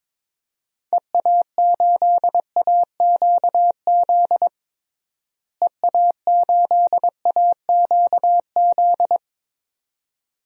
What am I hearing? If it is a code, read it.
Morse code, heard as a keyed tone, about 21 wpm, 700 Hz. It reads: EA8AQZ EA8AQZ
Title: DXCC-100 Call Signs - 6 Characters - Repeated 2x - 22wpm